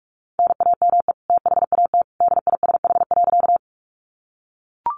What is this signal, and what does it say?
Morse code, heard as a keyed tone, about 44 wpm, 700 Hz. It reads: NAME THAT DISH.
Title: Sentences from Top 1300 Words 45wpm